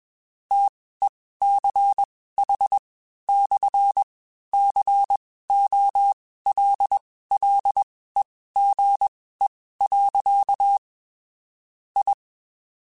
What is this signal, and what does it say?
Morse code, heard as a keyed tone, about 21 wpm, spaced as for 15 wpm, 800 Hz. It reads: TECH/COLLEGE. I
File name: QSOs / QSOs-mp3s / WD0BGP 1@20